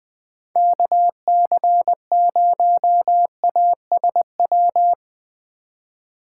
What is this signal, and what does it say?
Morse code, heard as a keyed tone, about 20 wpm, 700 Hz. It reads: KC0ASW
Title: USA Call Signs - 6 Characters - 3x Character Spacing - 20wpm